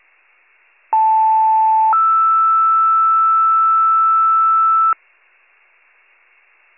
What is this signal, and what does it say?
Two-tone page, heard as a one-second tone, then a three-second tone, about 880 then 1320 Hz. It is heard from 0.9 to 4.9 s.